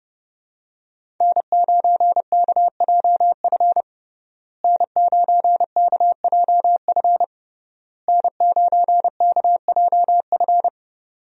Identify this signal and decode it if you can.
Morse code, heard as a keyed tone, about 30 wpm, 700 Hz. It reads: N9KJF N9KJF N9KJF
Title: USA Call Signs - 5 Characters - Repeated 3x - 30wpm